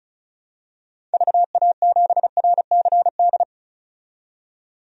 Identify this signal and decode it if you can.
Morse code, heard as a keyed tone, about 36 wpm, 700 Hz. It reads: VA7RCD
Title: DXCC-100 Call Signs - 6 Characters - 35wpm